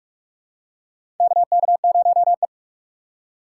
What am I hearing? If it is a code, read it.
Morse code, heard as a keyed tone, about 46 wpm, 700 Hz. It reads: KK0E
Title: USA Call Signs - 4 Characters - 2x Character Spacing - 45wpm